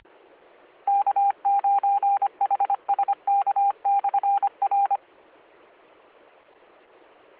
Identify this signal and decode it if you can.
Morse code, heard as a keyed tone, about 25 wpm, 800 Hz. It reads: K9HSK/R